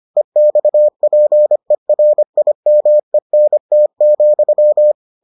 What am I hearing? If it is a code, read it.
Morse code, heard as a keyed tone, about 25 wpm, 600 Hz. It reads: EXPERIMENT,